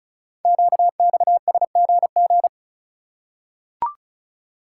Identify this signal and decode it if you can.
Morse code, heard as a keyed tone, about 36 wpm, 700 Hz. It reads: QXSGG
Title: Code Groups - 5 Letter:Numbers - 35wpm